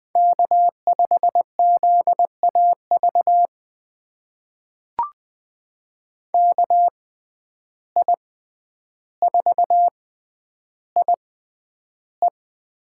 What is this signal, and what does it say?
Morse code, heard as a keyed tone, about 20 wpm, spaced as for 10 wpm, 700 Hz. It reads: K5ZAV KI4IE
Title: USA Call Signs - 5 Characters - 6x Character Spacing - 20wpm